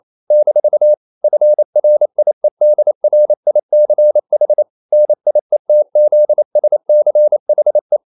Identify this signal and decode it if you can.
Morse code, heard as a keyed tone, about 28 wpm, 600 Hz. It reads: - FRIEDRICH NIETZSCHE
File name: CW / Practice / EnglishQuotations0228